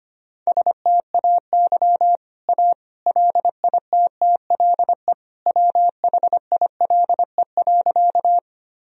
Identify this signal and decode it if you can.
Morse code, heard as a keyed tone, about 25 wpm, 700 Hz. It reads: STAY A LITTLE WHILE.
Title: Sentences from Top 1000 Words - Review Part 1 - 25wpm